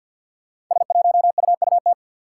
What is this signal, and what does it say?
Morse code, heard as a keyed tone, about 48 wpm, 700 Hz. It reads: S1UUT
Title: Five Challenging Letter-Number Combinations 50wpm